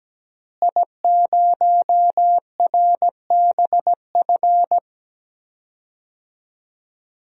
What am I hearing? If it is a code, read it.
Morse code, heard as a keyed tone, about 17 wpm, 700 Hz. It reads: I0RBF
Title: DXCC-100 Call Signs - 5 Characters - Repeated 2x - 17wpm